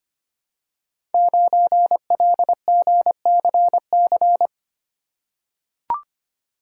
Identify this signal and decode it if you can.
Morse code, heard as a keyed tone, about 25 wpm, 700 Hz. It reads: 9LGCC
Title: Code Groups - 5 Letter:Numbers - 25wpm